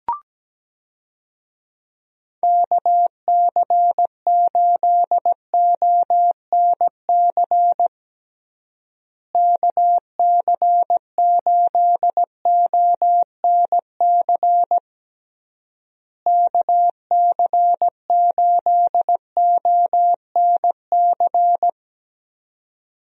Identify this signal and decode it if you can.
Morse code, heard as a keyed tone, about 17 wpm, 700 Hz. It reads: KC8ONC KC8ONC KC8ONC
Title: USA Call Signs - 6 Characters - Repeated 3x - 17wpm